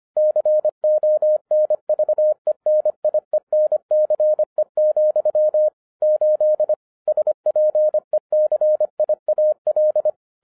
Morse code, heard as a keyed tone, about 25 wpm, 610 Hz. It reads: CONVENIENCE, 8 SPECIAL
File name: 2011-10-mp3 / n09-CW TO PS2 CONVERTER 25wpm